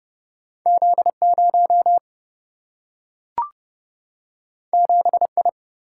Morse code, heard as a keyed tone, about 30 wpm, 700 Hz. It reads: Z 0 7 I
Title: Single-Letter Single-Number 30wpm